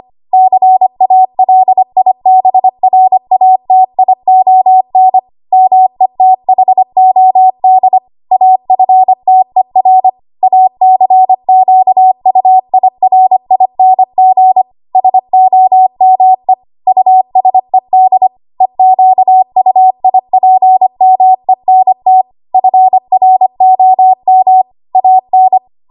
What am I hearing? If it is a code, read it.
Morse code, heard as a keyed tone, about 25 wpm, 750 Hz. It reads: CALIBRATION METHOD AFTER ACQUIRING SOME USED EQUIPMENT FROM AN
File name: Archive / 25 WPM / 25WPM